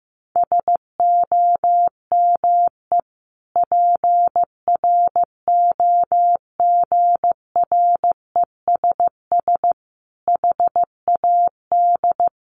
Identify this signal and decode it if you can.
Morse code, heard as a keyed tone, about 15 wpm, 700 Hz. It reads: SOME PROGRESS HAD
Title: Sentences from Top 1400 Words 15wpm